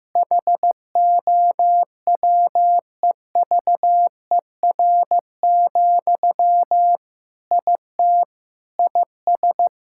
Morse code, heard as a keyed tone, about 15 wpm, 700 Hz. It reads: HOWEVER, IT IS
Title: Sentences from Top 1600 Words 15wpm